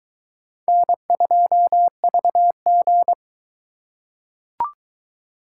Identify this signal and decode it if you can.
Morse code, heard as a keyed tone, about 22 wpm, 700 Hz. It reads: N2VG